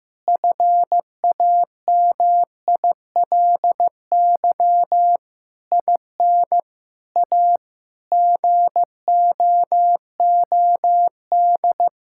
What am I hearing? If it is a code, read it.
Morse code, heard as a keyed tone, about 15 wpm, 700 Hz. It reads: FAMILY IN A GOOD
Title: Sentences from Top 1000 Words - Review Part 5 - 15wpm